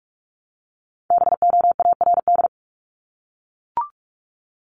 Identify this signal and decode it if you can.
Morse code, heard as a keyed tone, about 44 wpm, 700 Hz. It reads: BOARD
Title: Top 501-750 Words 45wpm